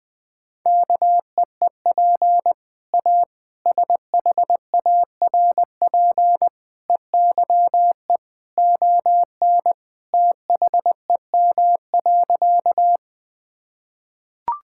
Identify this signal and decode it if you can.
Morse code, heard as a keyed tone, about 20 wpm, 700 Hz. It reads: KEEP A SHARP EYE ON THEM.
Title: Sentences from Top 1600 Words 20wpm